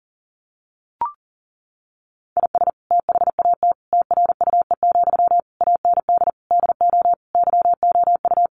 Morse code, heard as a keyed tone, about 37 wpm, 700 Hz. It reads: IS THAT TRUE, AND DO YOU
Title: Sentences from Top 1200 Words 40wpm